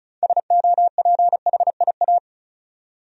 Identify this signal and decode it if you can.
Morse code, heard as a keyed tone, about 34 wpm, 700 Hz. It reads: SOPHIA
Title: USA Common Names - 35wpm